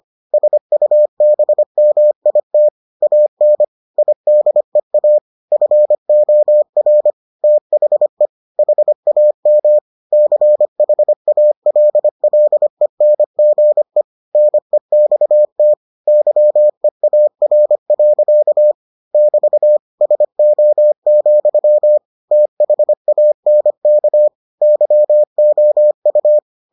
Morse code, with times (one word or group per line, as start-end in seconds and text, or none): 0.3-2.7 SUBMIT
3.0-3.6 AN
4.0-5.2 IDEA
5.5-7.1 FOR
7.4-8.3 THE
8.6-9.8 HAM
10.1-14.0 CHALLENGE
14.3-15.7 NEXT
16.1-18.7 YEAR.
19.1-19.8 <BT>
20.0-22.0 SO,
22.3-24.3 THANK
24.6-26.4 YOU